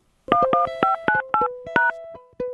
DTMF, heard as telephone touch tones, tones of about 140 ms. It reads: 11598#